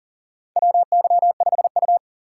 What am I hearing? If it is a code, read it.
Morse code, heard as a keyed tone, about 40 wpm, 700 Hz. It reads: WY5U
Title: USA Call Signs - 4 Characters - 6x Character Spacing - 40wpm